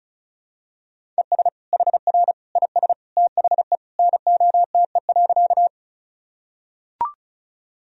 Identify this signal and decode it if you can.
Morse code, heard as a keyed tone, about 36 wpm, 700 Hz. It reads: ES HR IS THE NOTE.
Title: Sentences from Top 500 Words and Abbreviations 35wpm